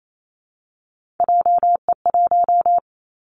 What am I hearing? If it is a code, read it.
Morse code, heard as a keyed tone, about 28 wpm, 700 Hz. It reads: JE1
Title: Callsign Prefixes for Top 25 DXCC - 28wpm